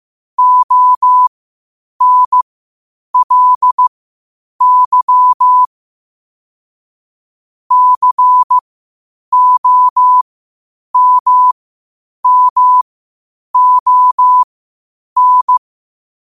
Morse code, heard as a keyed tone, about 15 wpm, spaced as for 8 wpm, 1000 Hz. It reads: ONLY COMMON
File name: qotdfiles / 8wpm